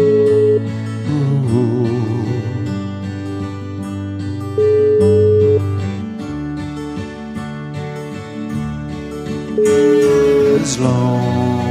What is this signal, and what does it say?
Ringback tone (Europe, 425 Hz), with noise.